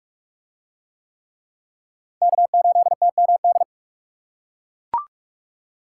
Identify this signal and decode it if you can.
Morse code, heard as a keyed tone, about 41 wpm, 700 Hz. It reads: K8TMD